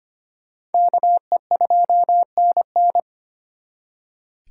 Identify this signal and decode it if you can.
Morse code, heard as a keyed tone, about 30 wpm, 700 Hz. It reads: KE2NN
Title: Speed Racing - USA Call Signs 25wpm